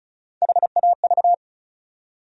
Morse code, heard as a keyed tone, about 34 wpm, 700 Hz. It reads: HAV
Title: USA Call Sign Postfixes - 3 Letters 35wpm